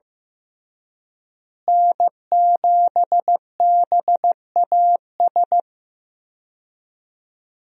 Morse code, heard as a keyed tone, about 15 wpm, 700 Hz. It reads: N7BAS
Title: USA Call Signs - 5 Characters - Repeated 2x - 15wpm